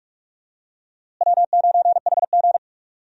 Morse code, heard as a keyed tone, about 46 wpm, 700 Hz. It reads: W9HG
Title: USA Call Signs - 4 Characters - Repeated 3x - 45wpm